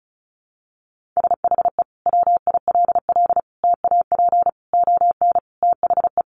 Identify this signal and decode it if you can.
Morse code, heard as a keyed tone, about 34 wpm, 700 Hz. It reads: SHE WILL TAP ON THE
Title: Sentences from Top 1400 Words 35wpm